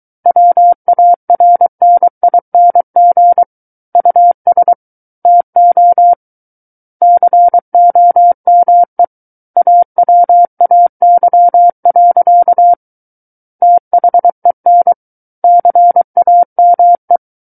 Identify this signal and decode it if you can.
Morse code, heard as a keyed tone, about 23 wpm, 700 Hz. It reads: WARNING US TO COME AWAY. THEN CAME
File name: Book One / 14-Chapter 14 - 23 WPM